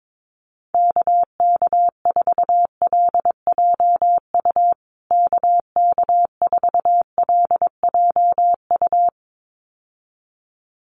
Morse code, heard as a keyed tone, about 22 wpm, 700 Hz. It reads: KK4LJU KK4LJU
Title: USA Call Signs Repeated 2x 22wpm